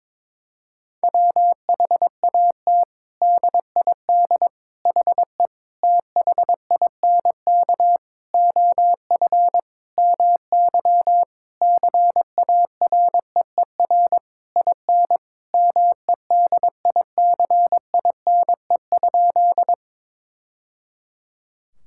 Morse code, heard as a keyed tone, about 22 wpm, 700 Hz. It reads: WHAT DID HE THINK OF MY CAREER IN MEDICINE?
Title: Sentences from Top 1000 Words - Review Part 6 - 22wpm